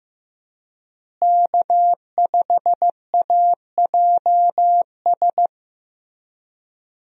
Morse code, heard as a keyed tone, about 15 wpm, 700 Hz. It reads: K5AJS
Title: USA Call Signs - 5 Characters - Repeated 3x - 15wpm